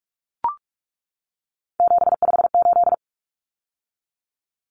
Morse code, heard as a keyed tone, about 44 wpm, 700 Hz. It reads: 758
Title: Numbers - 3-Digit 45wpm